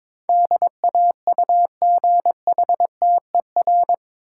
Morse code, heard as a keyed tone, about 22 wpm, 700 Hz. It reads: DAUGHTER
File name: practice / 22wpm / Eight Letter Words - 22wpm